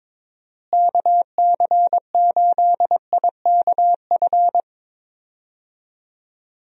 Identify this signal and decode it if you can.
Morse code, heard as a keyed tone, about 22 wpm, 700 Hz. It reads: KC8IKF
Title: USA Call Signs - 6 Characters - Repeated 3x - 22wpm